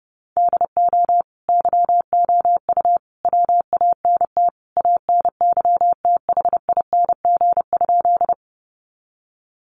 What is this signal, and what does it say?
Morse code, heard as a keyed tone, about 30 wpm, 700 Hz. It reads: DO YOU WANT ANYTHING?
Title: Sentences from Top 1000 Words 30wpm